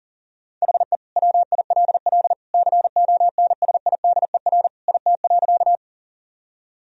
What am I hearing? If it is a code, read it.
Morse code, heard as a keyed tone, about 40 wpm, 700 Hz. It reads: HE WILL CONSIDER IT.
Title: Sentences from Top 1000 Words - Review Part 4 - 40wpm